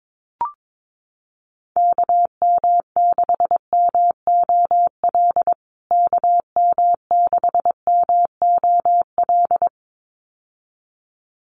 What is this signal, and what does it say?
Morse code, heard as a keyed tone, about 22 wpm, 700 Hz. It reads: KM6MOL KM6MOL
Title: USA Call Signs Repeated 2x 22wpm